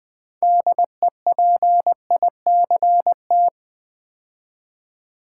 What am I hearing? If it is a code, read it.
Morse code, heard as a keyed tone, about 20 wpm, 700 Hz. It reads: DEPICT
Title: 6 Letter Words - First 3 Letters Given - 20wpm